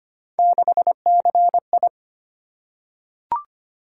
Morse code, heard as a keyed tone, about 25 wpm, 700 Hz. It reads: 6CI
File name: Three Letter-Number Combinations - 25wpm